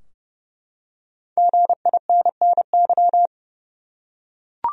Morse code, heard as a keyed tone, about 30 wpm, 700 Hz. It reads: GINNY
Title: USA Common Names - 30wpm